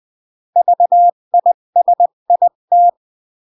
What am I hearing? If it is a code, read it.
Morse code, heard as a keyed tone, about 20 wpm, 700 Hz. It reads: VISIT
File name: CW / jodorowsky / 82 commandments 20-15